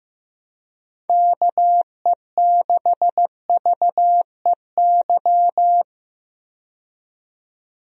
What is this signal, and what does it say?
Morse code, heard as a keyed tone, about 15 wpm, 700 Hz. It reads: KE6VEY
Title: USA Call Signs - 6 Characters - 15wpm